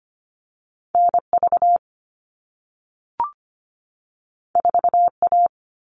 Morse code, heard as a keyed tone, about 25 wpm, 700 Hz. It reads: NV 4A
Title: All Cut Numbers - 2-Digit 25wpm